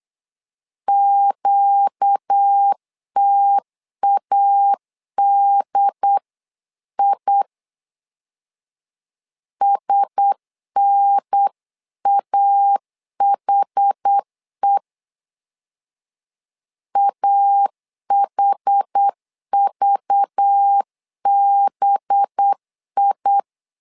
Morse code, heard as a keyed tone, about 8 wpm, 800 Hz. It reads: QTADI SNAHE AHVBI
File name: LEZIONI / 5 LBVC / 5 LBV 8wpm 2837